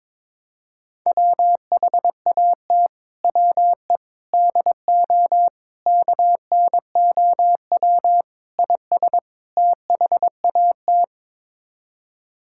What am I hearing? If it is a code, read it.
Morse code, heard as a keyed tone, about 22 wpm, 700 Hz. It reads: WHAT WE DO KNOW IS THAT
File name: practice / 22wpm / Sets of 6 Words - 22wpm